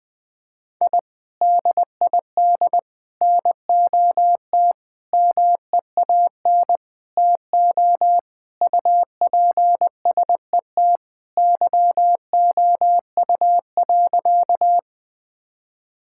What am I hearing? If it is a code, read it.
Morse code, heard as a keyed tone, about 21 wpm, 700 Hz. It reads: I DID NOT MEAN TO UPSET YOU.
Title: Sentences with 5-Word Phrases - 20wpm